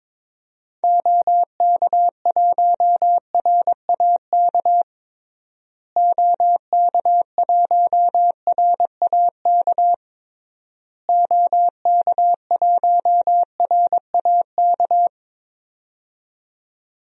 Morse code, heard as a keyed tone, about 22 wpm, 700 Hz. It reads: OK1RAK OK1RAK OK1RAK
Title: DXCC-100 Call Signs - 6 Characters - Repeated 3x - 22wpm